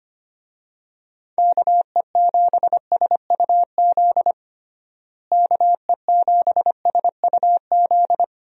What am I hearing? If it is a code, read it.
Morse code, heard as a keyed tone, about 25 wpm, 700 Hz. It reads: KE7SUZ KE7SUZ
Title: USA Call Signs - 6 Characters - Repeated 2x - 25wpm